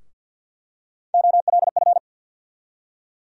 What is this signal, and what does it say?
Morse code, heard as a keyed tone, about 49 wpm, 700 Hz. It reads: OLF
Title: USA Call Sign Postfixes - 3 Letters 50wpm